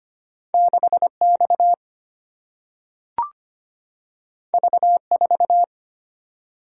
Morse code, heard as a keyed tone, about 25 wpm, 700 Hz. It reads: 6X V4
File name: Two Letter-Number Combinations - 25wpm